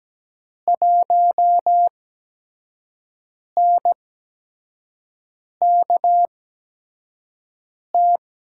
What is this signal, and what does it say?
Morse code, heard as a keyed tone, about 17 wpm, spaced as for 5 wpm, 700 Hz. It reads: 1NKT
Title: USA Call Signs - 5 Characters - 8x Character Spacing - 17wpm